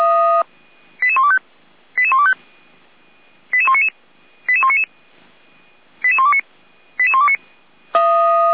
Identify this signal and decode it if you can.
Five-tone selective call (ZVEI), heard as quick runs of five tones, tones of about 70 ms. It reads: 88125 88125 88199 88199 88129 88129